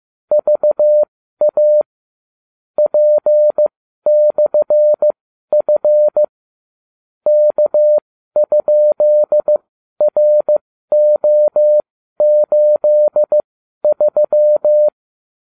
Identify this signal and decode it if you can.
Morse code, heard as a keyed tone, about 15 wpm, spaced as for 12 wpm, 600 Hz. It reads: VA P/F K?RO83